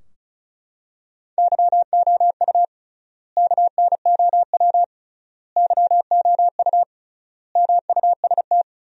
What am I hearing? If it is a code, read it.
Morse code, heard as a keyed tone, about 34 wpm, 700 Hz. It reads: YOU KNOW YOU MUST
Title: 3x Word-Spacing - Sentences from Top 500 Words - 35wpm